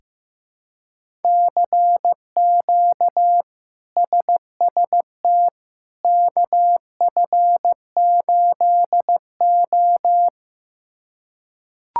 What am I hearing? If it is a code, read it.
Morse code, heard as a keyed tone, about 15 wpm, 700 Hz. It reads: CQ SST KF8O
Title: K1USN SST QSO Practice - 15wpm